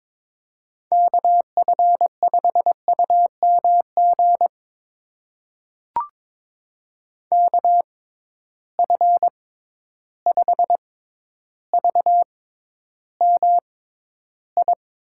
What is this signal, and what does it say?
Morse code, heard as a keyed tone, about 22 wpm, spaced as for 11 wpm, 700 Hz. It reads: KF5UMG KF5VMI
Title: USA Call Signs - 6 Characters - 6x Character Spacing - 22wpm